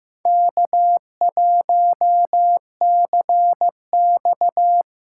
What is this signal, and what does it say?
Morse code, heard as a keyed tone, about 15 wpm, 700 Hz. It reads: K1CX
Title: K1USN SST QSO Practice - 15wpm